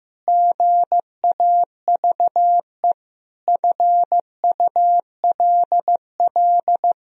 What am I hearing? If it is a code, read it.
Morse code, heard as a keyed tone, about 15 wpm, 700 Hz. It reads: GAVE FULL
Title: Sentences from Top 1000 Words - Review Part 3 - 15wpm